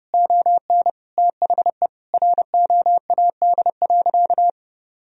Morse code, heard as a keyed tone, about 32 wpm, 700 Hz. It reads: ON THE ROAD.
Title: Sentences from Top 1000 Words - Review Part 2 - 30wpm